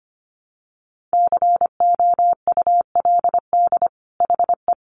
Morse code, heard as a keyed tone, about 25 wpm, 700 Hz. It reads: COULD HE
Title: Sentences from Top 1300 Words 25wpm